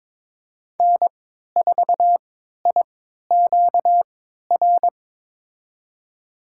Morse code, heard as a keyed tone, about 21 wpm, spaced as for 12 wpm, 700 Hz. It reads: N4IQR
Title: USA Call Signs - 5 Characters - 3x Character Spacing - 22wpm